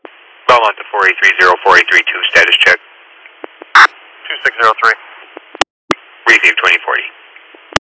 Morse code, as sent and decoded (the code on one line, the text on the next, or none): none